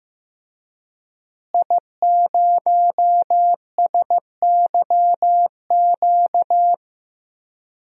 Morse code, heard as a keyed tone, about 15 wpm, 700 Hz. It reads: I0SYQ
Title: DXCC-100 Call Signs - 5 Characters - Repeated 3x - 15wpm